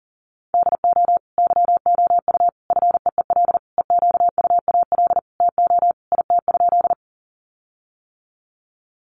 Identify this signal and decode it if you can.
Morse code, heard as a keyed tone, about 40 wpm, 700 Hz. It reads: DO YOU FEEL EQUAL TO IT?
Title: Sentences from Top 1400 Words 40wpm